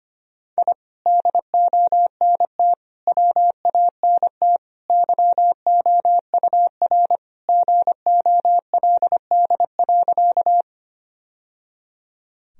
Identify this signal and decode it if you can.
Morse code, heard as a keyed tone, about 25 wpm, 700 Hz. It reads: I DONT WANT YOUR GOLD.
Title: Sentences from Top 1000 Words - Review Part 3 - 25wpm